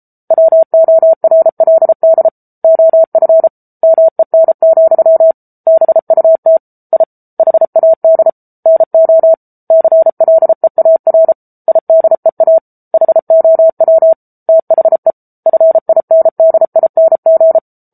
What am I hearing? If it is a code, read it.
Morse code, heard as a keyed tone, about 34 wpm, 650 Hz. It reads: WORLD OF MEN, BUT I HAD NO CLEAR IDEA HOW THE FINDING
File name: Book Two / 07-Chapter 7 - 33 WPM